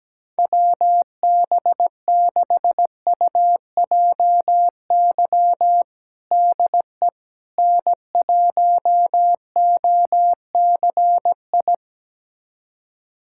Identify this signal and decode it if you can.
Morse code, heard as a keyed tone, about 17 wpm, 700 Hz. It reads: WB6UJY DE N1OCI
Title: USA Callsign DE Callsign 17wpm